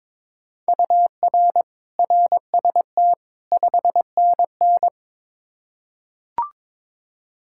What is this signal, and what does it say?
Morse code, heard as a keyed tone, about 22 wpm, 700 Hz. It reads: UR RST 5NN
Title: Practice QSO - Exchange 1 - Beginning - 22wpm